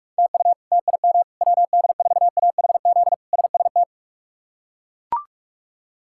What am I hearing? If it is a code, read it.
Morse code, heard as a keyed tone, about 42 wpm, 700 Hz. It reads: TU TIM WD4AHZ SST
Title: K1USN SST QSO Practice - 45wpm